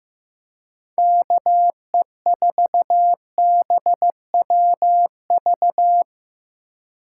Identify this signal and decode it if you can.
Morse code, heard as a keyed tone, about 15 wpm, 700 Hz. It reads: KE4BWV
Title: USA Call Signs - 4x Character Spacing - 15wpm